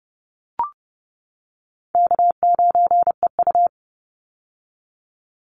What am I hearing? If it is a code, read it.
Morse code, heard as a keyed tone, about 31 wpm, 700 Hz. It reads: K9EU
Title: Call Signs - 30wpm